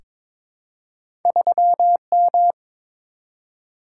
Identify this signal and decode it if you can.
Morse code, heard as a keyed tone, about 22 wpm, 700 Hz. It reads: 3M
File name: practice / 22wpm / Two Letter-Number Combinations - 22wpm